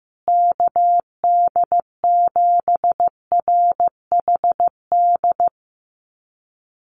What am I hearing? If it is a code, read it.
Morse code, heard as a keyed tone, about 15 wpm, 700 Hz. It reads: KD7RHD
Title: USA Call Signs Repeated 3x 15wpm